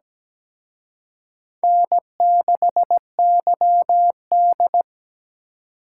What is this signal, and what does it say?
Morse code, heard as a keyed tone, about 20 wpm, 700 Hz. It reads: N6YD
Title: Speed Racing - USA 1x2 Call Signs 17wpm